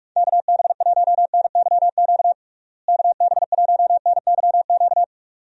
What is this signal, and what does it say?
Morse code, heard as a keyed tone, about 46 wpm, 700 Hz. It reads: KB1NYQ KB1NYQ
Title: USA Call Signs - 6 Characters - Repeated 2x - 45wpm